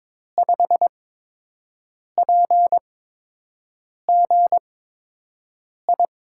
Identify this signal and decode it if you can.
Morse code, heard as a keyed tone, about 22 wpm, spaced as for 6 wpm, 700 Hz. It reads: 5PGI
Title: USA Call Signs - 6 Characters - 8x Character Spacing - 22wpm